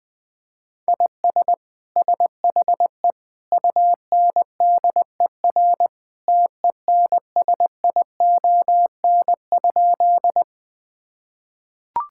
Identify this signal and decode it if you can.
Morse code, heard as a keyed tone, about 20 wpm, 700 Hz. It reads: IS SHE UNDER TENSION?
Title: Sentences from Top 1600 Words 20wpm